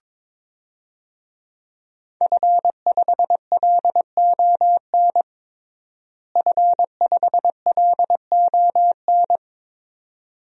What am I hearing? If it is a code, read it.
Morse code, heard as a keyed tone, about 22 wpm, 700 Hz. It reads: F5LON F5LON
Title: DXCC-100 Call Signs - 5 Characters - Repeated 2x - 22wpm